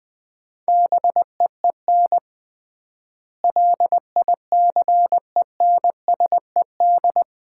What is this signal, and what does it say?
Morse code, heard as a keyed tone, about 20 wpm, 700 Hz. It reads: BEEN LICENSED
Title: Common QSO Phrases - 3x Word-Spacing - 20wpm